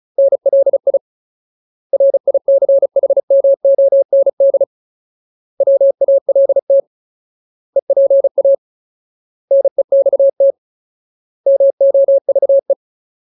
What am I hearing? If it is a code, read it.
Morse code, heard as a keyed tone, about 34 wpm, 550 Hz. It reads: NLI RICHMOND WALT EPA NEXT MOVE
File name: Ultimate Word List / 35WPM Ultimate